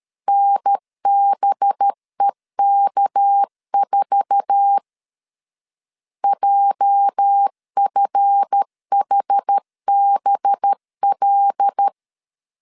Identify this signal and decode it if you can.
Morse code, heard as a keyed tone, about 13 wpm, 800 Hz. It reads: NBEK4 JFHBL
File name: LEZIONI / 8 YJ45 / 8 YJ4 12wpm 2151